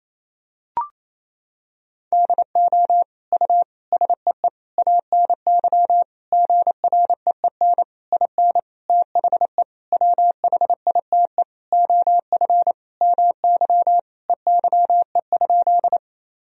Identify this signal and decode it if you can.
Morse code, heard as a keyed tone, about 28 wpm, 700 Hz. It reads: DO U SEE ANY GREEN IN THE WHITE OF MY EYE?
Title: Sentences from Top 500 Words and Abbreviations 28wpm